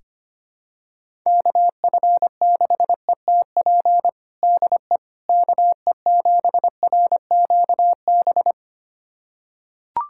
Morse code, heard as a keyed tone, about 25 wpm, 700 Hz. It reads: KF6ETP DE KE7RQB
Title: USA Callsign DE Callsign - 6 Characters - 25wpm